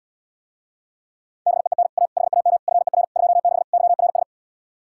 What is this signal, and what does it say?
Morse code, heard as a keyed tone, about 80 wpm, 700 Hz. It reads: BEEN A HAM FER 26 YRS